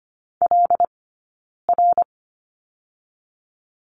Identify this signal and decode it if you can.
Morse code, heard as a keyed tone, about 26 wpm, 700 Hz. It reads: L R
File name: Lesson 15d - Number 5 - Two Characters 25wpm